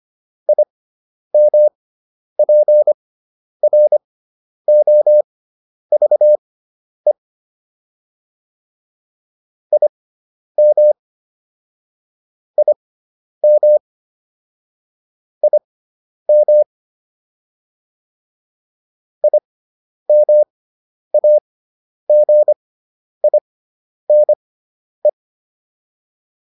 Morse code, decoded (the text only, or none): IMPROVE IM IM IM IMAGINE